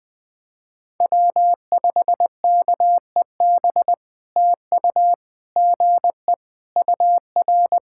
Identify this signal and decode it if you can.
Morse code, heard as a keyed tone, about 20 wpm, 700 Hz. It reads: W5KEB TU GE UR
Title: POTA QSO Practice - 20wpm